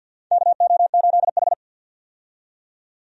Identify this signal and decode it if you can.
Morse code, heard as a keyed tone, about 49 wpm, 700 Hz. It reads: KO8H
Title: USA Call Signs - 4 Characters - 50wpm